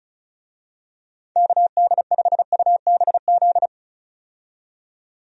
Morse code, heard as a keyed tone, about 36 wpm, 700 Hz. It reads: KD5UBZ
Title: USA Call Signs - 6 Characters - 35wpm